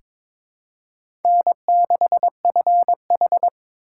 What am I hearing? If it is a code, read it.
Morse code, heard as a keyed tone, about 22 wpm, 700 Hz. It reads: N6FH
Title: USA Call Signs - 4 Characters - 2x Character Spacing - 22wpm